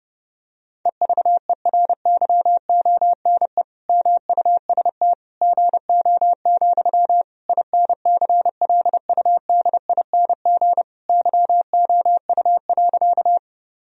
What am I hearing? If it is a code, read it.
Morse code, heard as a keyed tone, about 30 wpm, 700 Hz. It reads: EVERYONE MUST GO, INCLUDING YOU.
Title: Sentences from Top 1000 Words - Review Part 4 - 30wpm